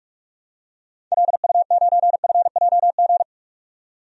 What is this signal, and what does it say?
Morse code, heard as a keyed tone, about 46 wpm, 700 Hz. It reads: LU9FJG
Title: DXCC-100 Call Signs - 6 Characters - 45wpm